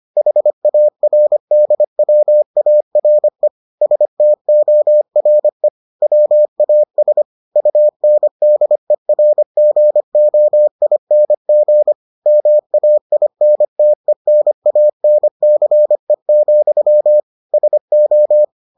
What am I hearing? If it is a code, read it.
Morse code, heard as a keyed tone, about 25 wpm, 600 Hz. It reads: HARDWARE STORE WAS UNDERGOING MAINTENANCE, SO